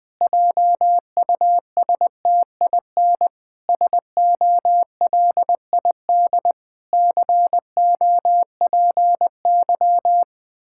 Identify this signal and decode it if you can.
Morse code, heard as a keyed tone, about 20 wpm, 700 Hz. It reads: JUSTIN SOLID COPY
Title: Practice QSO - Exchange 3 - More Info - 20wpm